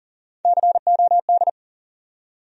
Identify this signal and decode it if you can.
Morse code, heard as a keyed tone, about 40 wpm, 700 Hz. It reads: COD
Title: USA Call Sign Postfixes - 3 Letters 40wpm